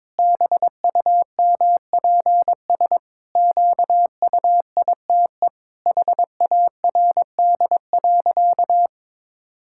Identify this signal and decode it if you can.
Morse code, heard as a keyed tone, about 21 wpm, 700 Hz. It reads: BUMPS QUITE HARD.
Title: Pangram Challenge v2 - 22wpm